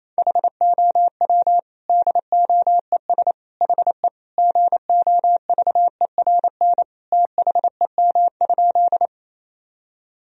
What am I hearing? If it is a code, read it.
Morse code, heard as a keyed tone, about 28 wpm, 700 Hz. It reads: HOW DOES HE GOVERN THEM?
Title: Sentences from Top 1000 Words - Review Part 2 - 28wpm